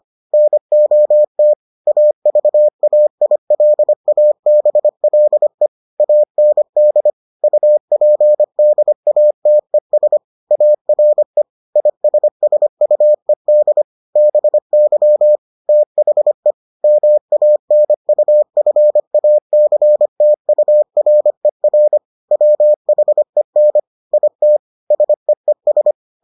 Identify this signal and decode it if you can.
Morse code, heard as a keyed tone, about 25 wpm, 600 Hz. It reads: NOT AVAILABLE AND UPDATES ARE ISSUED BY THE MANUFACTURER WHEN IT SEES